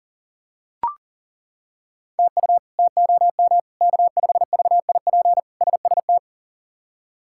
Morse code, heard as a keyed tone, about 40 wpm, 700 Hz. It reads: TU TOM K5VIP SST
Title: K1USN SST QSO Practice - 40wpm